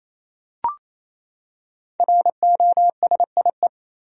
Morse code, heard as a keyed tone, about 28 wpm, 700 Hz. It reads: ROSIE
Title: USA Common Names - Max 5 Letters - 28wpm